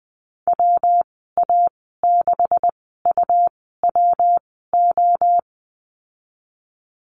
Morse code, heard as a keyed tone, about 19 wpm, spaced as for 14 wpm, 700 Hz. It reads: WA6UWO
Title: 128 Lesson 21g - Number 6 - Call Signs 20wpm